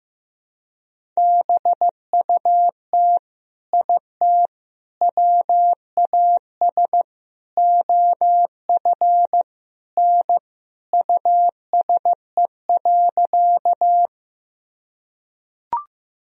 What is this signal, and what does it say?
Morse code, heard as a keyed tone, about 15 wpm, 700 Hz. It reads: BUT IT WAS OF N USE.
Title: Sentences from Top 100 Words and Abbreviations 15wpm